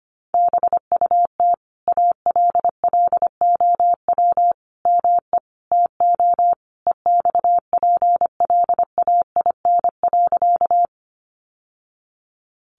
Morse code, heard as a keyed tone, about 25 wpm, 700 Hz. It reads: BUT ALLOW ME TO EXPLAIN.
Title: Sentences from Top 1100 Words 25wpm